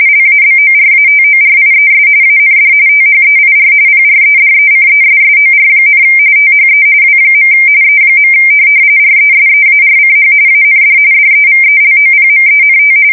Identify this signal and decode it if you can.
RTTY, about 45.45 baud. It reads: LABORATORIES AND THEIR PREDECESSOR, THE W. E. RESEARCH AND DEVELOPMENT
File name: fsk170 / teletype-corp